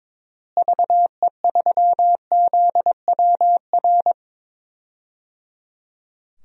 Morse code, heard as a keyed tone, about 22 wpm, 700 Hz. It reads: VE3ZWR
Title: DXCC-100 Call Signs - 6 Characters - Repeated 3x - 22wpm